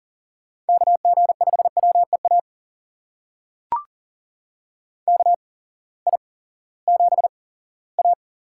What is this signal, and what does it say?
Morse code, heard as a keyed tone, about 40 wpm, spaced as for 20 wpm, 700 Hz. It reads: KG5WEA KI7A
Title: USA Call Signs - 6 Characters - 8x Character Spacing - 40wpm